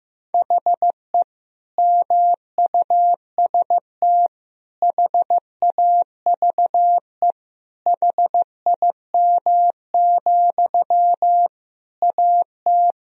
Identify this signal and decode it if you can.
Morse code, heard as a keyed tone, about 16 wpm, 700 Hz. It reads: HE MUST HAVE HIM, AT
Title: Sentences from Top 1000 Words - Review Part 4 - 15wpm